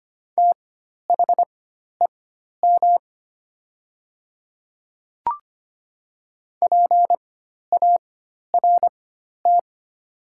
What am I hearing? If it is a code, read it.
Morse code, heard as a keyed tone, about 25 wpm, spaced as for 12 wpm, 700 Hz. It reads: THEM PART
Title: 4x Character Spacing - Top 200 Words - 25wpm